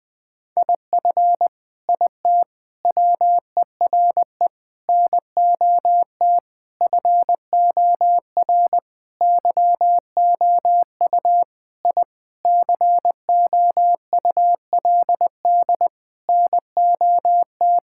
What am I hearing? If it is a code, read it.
Morse code, heard as a keyed tone, about 21 wpm, 700 Hz. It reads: IF IT WERE NOT FOR YOU I COULD NOT
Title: Sentences with 5-Word Phrases - 20wpm